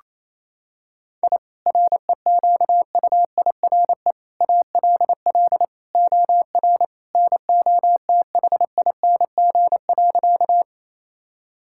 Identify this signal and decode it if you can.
Morse code, heard as a keyed tone, about 27 wpm, 700 Hz. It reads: I REQUIRE ALL OR NOTHING.
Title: Sentences from Top 1000 Words - Review Part 6 - 28wpm